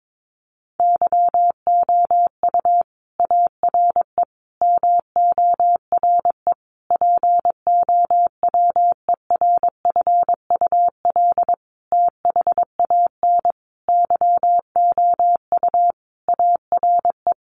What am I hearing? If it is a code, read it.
Morse code, heard as a keyed tone, about 22 wpm, 700 Hz. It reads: YOU ARE MORE POWERFUL THAN YOU ARE
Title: Sentences from Top 1300 Words 22wpm